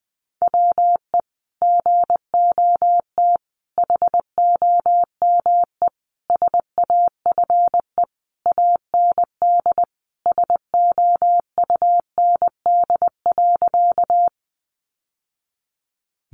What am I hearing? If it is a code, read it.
Morse code, heard as a keyed tone, about 20 wpm, 700 Hz. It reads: WE GOT HOME SAFE AND SOUND.
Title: Sentences from Top 900 Words 20wpm